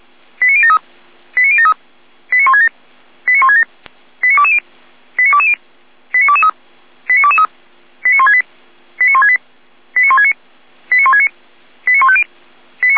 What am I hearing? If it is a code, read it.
Five-tone selective call (ZVEI), heard as quick runs of five tones, tones of about 70 ms. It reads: 89073 89073 89167 89167 89220 89220 89223 89223 89168 89168 89169 89169 89150